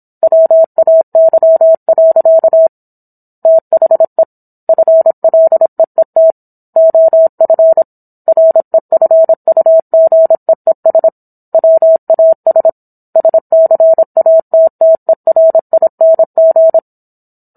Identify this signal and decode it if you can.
Morse code, heard as a keyed tone, about 27 wpm, 650 Hz. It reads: WAY. THE FLEET OF REFUGEES WAS SCATTERING